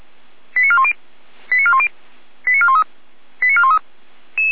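Five-tone selective call (ZVEI), heard as quick runs of five tones, tones of about 70 ms. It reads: 89410 89410 89413 89413
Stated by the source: ZVEI five-tone selective call received on SDR.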